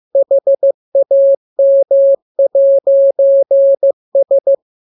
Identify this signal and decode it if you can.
Morse code, heard as a keyed tone, about 15 wpm, 550 Hz. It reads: HAM'S